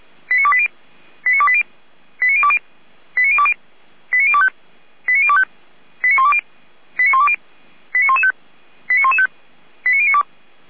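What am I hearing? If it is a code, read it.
Five-tone selective call (ZVEI), heard as quick runs of five tones, tones of about 70 ms. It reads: 89280 89280 89020 89020 89025 89025 89120 89120 89115 89115 89092
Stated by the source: ZVEI five-tone selective call 89262, received by SDR, from Feuerwehr Orlamünde's audio server.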